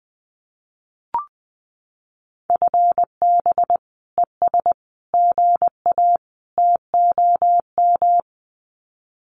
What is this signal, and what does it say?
Morse code, heard as a keyed tone, about 20 wpm, 700 Hz. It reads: FB ES GA TOM
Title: Common QSO Phrases Encore 20wpm